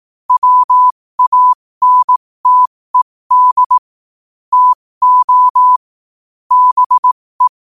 Morse code, heard as a keyed tone, about 18 wpm, spaced as for 15 wpm, 1000 Hz. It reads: WANTED TO BE
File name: qotdfiles / 15wpm